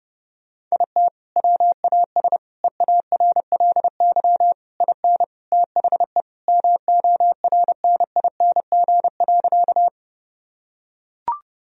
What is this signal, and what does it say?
Morse code, heard as a keyed tone, about 28 wpm, 700 Hz. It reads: IT WAS EARLY IN THE MORNING.
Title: Sentences from Top 1000 Words - Review Part 2 - 30wpm